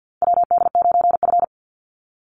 Morse code, heard as a keyed tone, about 48 wpm, 700 Hz. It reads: WD9F
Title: NA SOTA Chaser Callsigns 50wpm